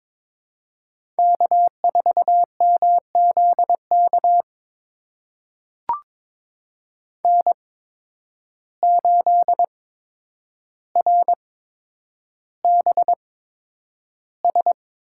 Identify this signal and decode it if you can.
Morse code, heard as a keyed tone, about 22 wpm, spaced as for 9 wpm, 700 Hz. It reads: K4MZK N8RBS
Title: USA Call Signs - 5 Characters - 8x Character Spacing - 22wpm